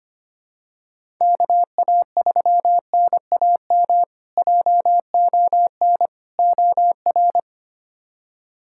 Morse code, heard as a keyed tone, about 25 wpm, 700 Hz. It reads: KA3NAM JON OR
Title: K1USN SST QSO Practice - 25wpm